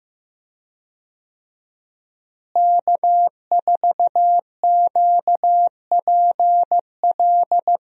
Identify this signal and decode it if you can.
Morse code, heard as a keyed tone, about 15 wpm, 700 Hz. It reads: K4QPL
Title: USA Call Signs - 5 Characters - Repeated 2x - 15wpm